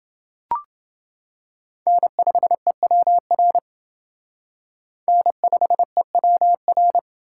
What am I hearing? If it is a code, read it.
Morse code, heard as a keyed tone, about 29 wpm, 700 Hz. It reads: N5EWR N5EWR